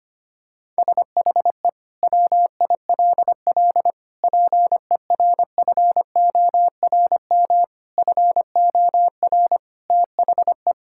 Morse code, heard as a keyed tone, about 25 wpm, 700 Hz. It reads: SHE WILL PERFORM FOR THE
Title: Sentences from Top 1000 Words - Review Part 5 - 25wpm